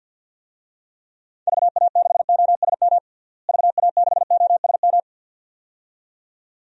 Morse code, heard as a keyed tone, about 53 wpm, 700 Hz. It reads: VA6OSM VA6OSM
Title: DXCC-100 Call Signs - 6 Characters - Repeated 2x - 50wpm